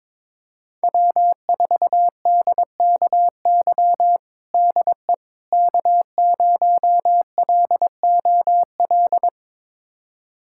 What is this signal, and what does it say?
Morse code, heard as a keyed tone, about 22 wpm, 700 Hz. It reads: W4DKY DE K0LOL
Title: USA Callsign DE Callsign - 5 Characters - 22wpm